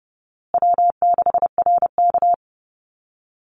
Morse code, heard as a keyed tone, about 30 wpm, 700 Hz. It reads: W6RK
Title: Call Signs Repeated 2x - 30wpm